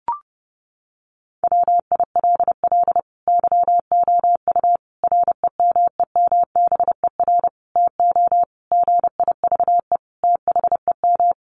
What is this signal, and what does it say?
Morse code, heard as a keyed tone, about 30 wpm, 700 Hz. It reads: WILL YOU REMEMBER TO GIVE THEM
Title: Sentences from Top 1100 Words 30wpm